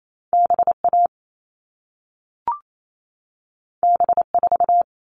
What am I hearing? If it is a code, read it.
Morse code, heard as a keyed tone, about 28 wpm, 700 Hz. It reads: BA B4
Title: All Cut Numbers - 2-Digit 28wpm